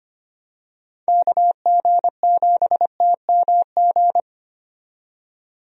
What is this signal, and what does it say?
Morse code, heard as a keyed tone, about 25 wpm, 700 Hz. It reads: KG7TMG
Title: USA Call Signs - 6 Characters - 25wpm